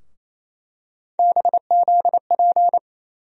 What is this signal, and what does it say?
Morse code, heard as a keyed tone, about 28 wpm, 700 Hz. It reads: BZP
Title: USA Call Sign Postfixes - 3 Letters 28wpm